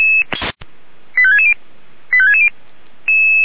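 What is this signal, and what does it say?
Five-tone selective call (ZVEI), heard as quick runs of five tones, tones of about 70 ms. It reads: 85660 85660
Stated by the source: ZVEI five-tone call received on an SDR.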